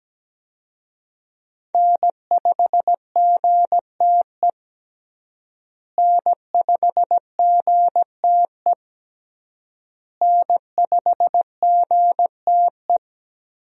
Morse code, heard as a keyed tone, about 17 wpm, 700 Hz. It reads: N5GTE N5GTE N5GTE
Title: USA Call Signs - 5 Characters - Repeated 3x - 17wpm